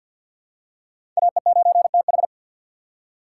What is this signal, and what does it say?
Morse code, heard as a keyed tone, about 50 wpm, 700 Hz. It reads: AE9TH
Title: USA Call Signs - 5 Characters - 3x Character Spacing - 50wpm